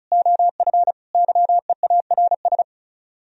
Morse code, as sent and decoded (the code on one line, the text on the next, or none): --- ..-. / -.-- . .- .-. ...
OF YEARS